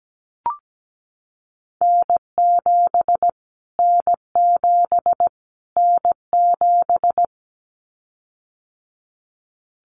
Morse code, heard as a keyed tone, about 17 wpm, 700 Hz. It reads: N7 N7 N7
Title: Callsign Prefixes for Top 25 DXCC - Repeated 3x - 17wpm